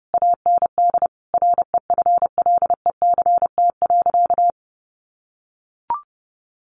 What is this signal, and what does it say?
Morse code, heard as a keyed tone, about 30 wpm, 700 Hz. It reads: AND REFLECT.
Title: Sentences from Top 1500 Words 30wpm